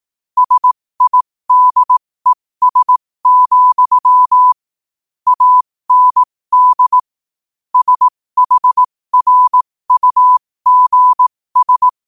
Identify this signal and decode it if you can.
Morse code, heard as a keyed tone, about 18 wpm, spaced as for 15 wpm, 1000 Hz. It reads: SIDES, AND SHRUGS
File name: qotdfiles / 15wpm